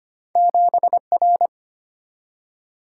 Morse code, heard as a keyed tone, about 25 wpm, 700 Hz. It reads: 7R
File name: Two Letter-Number Combinations - 25wpm